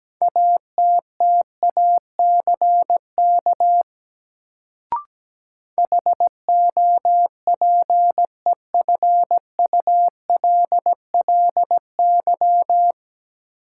Morse code, heard as a keyed tone, about 17 wpm, 700 Hz. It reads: ATTACK HOPEFULLY
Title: Sentences from Top 1500 Words - New Word Drill Alt - 17wpm